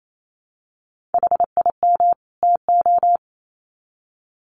Morse code, heard as a keyed tone, about 28 wpm, 700 Hz. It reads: HIM TO
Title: Sets of 2 Words 28wpm